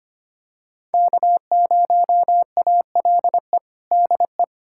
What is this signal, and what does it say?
Morse code, heard as a keyed tone, about 25 wpm, 700 Hz. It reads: K0ALE DE
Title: USA Callsign DE Callsign - 5 Characters - 25wpm